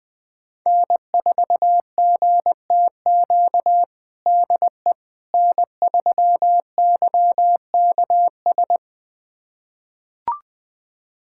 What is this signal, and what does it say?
Morse code, heard as a keyed tone, about 20 wpm, 700 Hz. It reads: N4GTQ DE N3YKS
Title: USA Callsign DE Callsign - 5 Characters - 20wpm